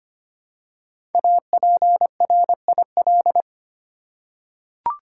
Morse code, heard as a keyed tone, about 25 wpm, 700 Hz. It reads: APRIL
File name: 5 Letter Words - First 3 Letters Given - 25wpm